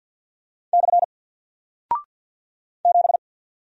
Morse code, heard as a keyed tone, about 50 wpm, 700 Hz. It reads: / 7
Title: Single Character (Letter:Number:Punctuation:Prosign) Alt 50wpm